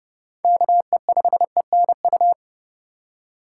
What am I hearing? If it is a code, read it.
Morse code, heard as a keyed tone, about 29 wpm, 700 Hz. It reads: KE5ENU
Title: USA Call Signs - 2x Character Spacing - 30wpm